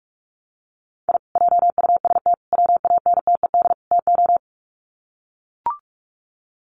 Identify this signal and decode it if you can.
Morse code, heard as a keyed tone, about 44 wpm, 700 Hz. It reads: I JUST WANTED TO
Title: Sets of 4 Words 45wpm